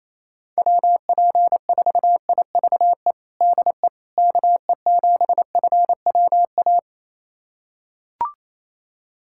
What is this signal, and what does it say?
Morse code, heard as a keyed tone, about 27 wpm, 700 Hz. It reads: WP4IVE DE KE7FWA
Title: USA Callsign DE Callsign 28wpm